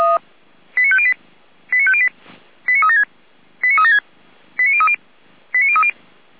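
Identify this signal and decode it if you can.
Five-tone selective call (ZVEI), heard as quick runs of five tones, tones of about 70 ms. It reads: 89508 89508 89376 89376 89030 89030
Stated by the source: Five-tone ZVEI selective call received on an SDR by feuerwehr-orlamuende (fire brigade Orlamünde).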